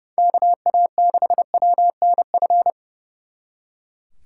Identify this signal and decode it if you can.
Morse code, heard as a keyed tone, about 37 wpm, 700 Hz. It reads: KA6WNF
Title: Speed Racing - USA 2x3 Call Signs 30wpm